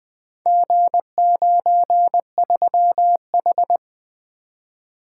Morse code, heard as a keyed tone, about 20 wpm, 700 Hz. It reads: G93H
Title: Four Letter-Number Combinations - 20wpm